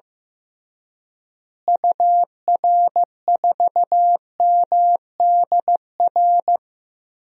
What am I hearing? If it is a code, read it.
Morse code, heard as a keyed tone, about 15 wpm, 700 Hz. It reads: UR4MDR
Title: DXCC-100 Call Signs - 6 Characters - Repeated 3x - 15wpm